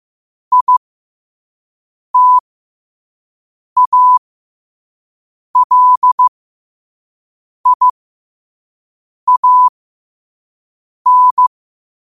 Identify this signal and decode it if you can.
Morse code, heard as a keyed tone, about 15 wpm, spaced as for 5 wpm, 1000 Hz. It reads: ITALIAN
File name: qotdfiles / 5wpm